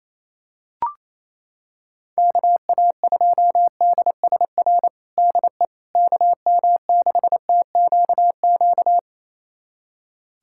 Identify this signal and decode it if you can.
Morse code, heard as a keyed tone, about 28 wpm, 700 Hz. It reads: KA2DSR DE KM6TQQ
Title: USA Callsign DE Callsign 28wpm